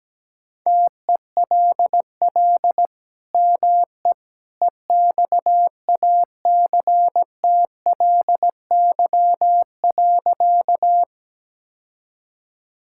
Morse code, heard as a keyed tone, about 17 wpm, 700 Hz. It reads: TELL ME EXACTLY.
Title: Sentences from Top 1000 Words - Review Part 6 - 17wpm